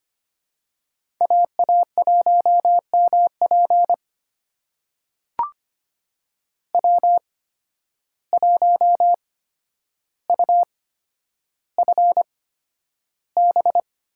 Morse code, heard as a keyed tone, about 25 wpm, spaced as for 11 wpm, 700 Hz. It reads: AA1MP W1UFB
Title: USA Call Signs - 5 Characters - 8x Character Spacing - 25wpm